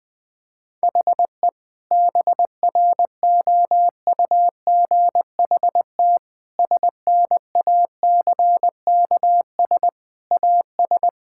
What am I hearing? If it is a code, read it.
Morse code, heard as a keyed tone, about 21 wpm, 700 Hz. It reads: HE BROUGHT SNACKS AS
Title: Sentences with 6-Word Phrases - 20wpm